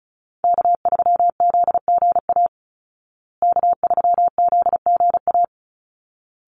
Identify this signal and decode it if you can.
Morse code, heard as a keyed tone, about 34 wpm, 700 Hz. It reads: K3ZGA K3ZGA
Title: Call Signs Repeated 3x - 35wpm